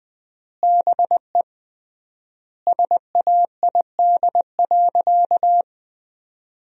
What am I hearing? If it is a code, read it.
Morse code, heard as a keyed tone, about 21 wpm, 700 Hz. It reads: BE SAID.
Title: 3x Word-Spacing - Sentences from Top 100 Words - 20wpm